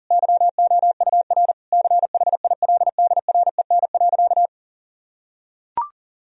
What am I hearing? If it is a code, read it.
Morse code, heard as a keyed tone, about 40 wpm, 700 Hz. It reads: YOUR CHILDREN.
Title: Sentences from Top 1000 Words - Review Part 3 - 40wpm